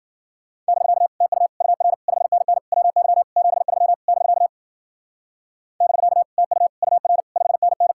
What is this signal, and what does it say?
Morse code, heard as a keyed tone, about 50 wpm, 700 Hz. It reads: <BK> TU UR 5NN WY 73 <BK> <BK> TU UR 5NN
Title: Speed-Racing Common QSO Phrases Encore - 40wpm